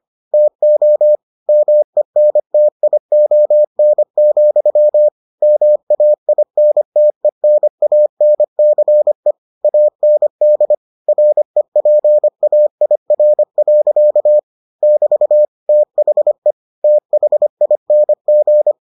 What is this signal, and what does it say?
Morse code, heard as a keyed tone, about 25 wpm, 600 Hz. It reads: TO MENTION, MAINTENANCE AND REPAIR. <BT> THE THING